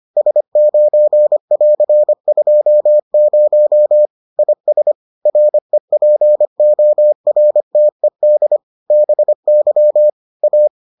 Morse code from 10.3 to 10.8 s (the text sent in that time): A